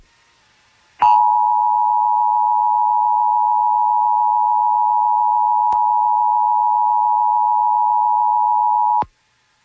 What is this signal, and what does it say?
Emergency Alert System (SAME) tones (the 853 and 960 Hz attention tone). It is heard from 1.0 to 9.1 s.